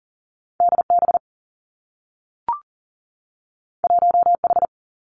Morse code, heard as a keyed tone, about 39 wpm, 700 Hz. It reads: D B 1 H
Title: Two Challenging Letter-Number Combinations 40wpm